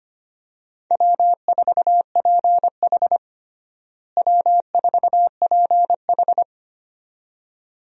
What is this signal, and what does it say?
Morse code, heard as a keyed tone, about 25 wpm, 700 Hz. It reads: W4PH W4PH
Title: USA Call Signs - 4 Characters - Repeated 2x - 25wpm